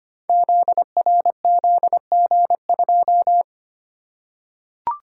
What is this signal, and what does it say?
Morse code, heard as a keyed tone, about 25 wpm, 700 Hz. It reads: ZRZG2
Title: Code Groups - 5 Letter:Numbers - 25wpm